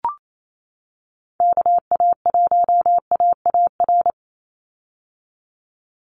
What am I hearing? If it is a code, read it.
Morse code, heard as a keyed tone, about 28 wpm, 700 Hz. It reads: KA1AAR
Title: USA Call Signs 28wpm